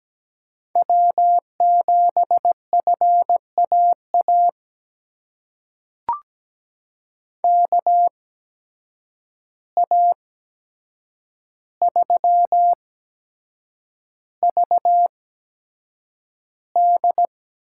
Morse code, heard as a keyed tone, about 17 wpm, spaced as for 7 wpm, 700 Hz. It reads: W7FAA KA3VD
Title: USA Call Signs - 8x Character Spacing - 17wpm